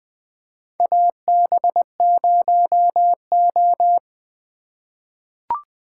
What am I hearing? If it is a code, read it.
Morse code, heard as a keyed tone, about 20 wpm, 700 Hz. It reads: AB0O
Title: 2023 POTA Activator Call Signs - 20wpm